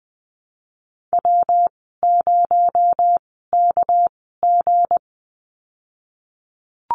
Morse code, heard as a keyed tone, about 20 wpm, spaced as for 14 wpm, 700 Hz. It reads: W0KG
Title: 219 Lesson 35g - Number 0 - Call Signs 20wpm